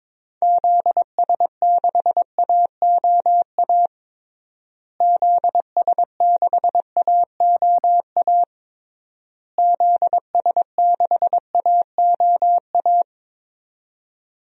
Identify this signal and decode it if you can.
Morse code, heard as a keyed tone, about 22 wpm, 700 Hz. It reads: ZS6AOA ZS6AOA ZS6AOA
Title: DXCC-100 Call Signs - 6 Characters - Repeated 3x - 22wpm